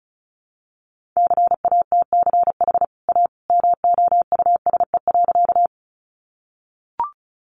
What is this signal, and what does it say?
Morse code, heard as a keyed tone, about 34 wpm, 700 Hz. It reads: CATCH A MOUSE.
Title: Sentences from Top 1500 Words 35wpm